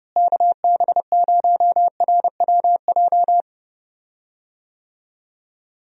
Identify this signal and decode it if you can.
Morse code, heard as a keyed tone, about 30 wpm, 700 Hz. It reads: KB0RWJ
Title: USA Call Signs - 6 Characters - Repeated 2x - 30wpm